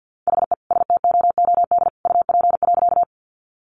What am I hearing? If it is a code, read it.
Morse code, heard as a keyed tone, about 50 wpm, 700 Hz. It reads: HE STOOD UP.
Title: Sentences from Top 500 Words 50wpm